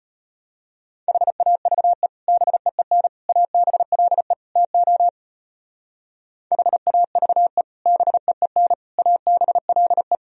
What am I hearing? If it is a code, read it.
Morse code, heard as a keyed tone, about 33 wpm, 700 Hz. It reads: HAVE BEEN ABLE TO HAVE BEEN ABLE